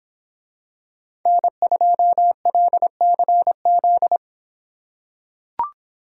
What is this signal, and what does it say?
Morse code, heard as a keyed tone, about 24 wpm, 700 Hz. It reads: N2LCZ